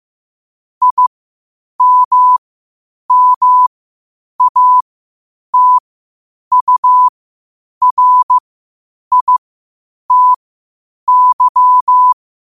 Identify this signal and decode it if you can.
Morse code, heard as a keyed tone, about 15 wpm, spaced as for 8 wpm, 1000 Hz. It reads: IMMATURITY
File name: qotdfiles / 8wpm